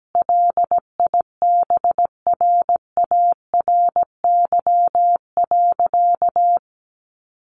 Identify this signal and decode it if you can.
Morse code, heard as a keyed tone, about 17 wpm, 700 Hz. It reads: LIBRARY.
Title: Sentences from Top 1100 Words 17wpm